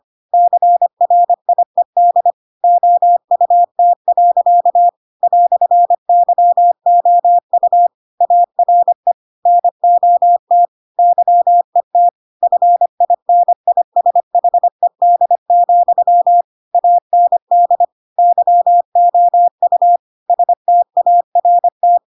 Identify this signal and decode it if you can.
Morse code, heard as a keyed tone, about 25 wpm, 700 Hz. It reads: CRIED OUT. "YOU ARE NOT YET FINISHED, AND YOU START
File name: Pinocchio / Pinocchio ch3 25 25 ik1hge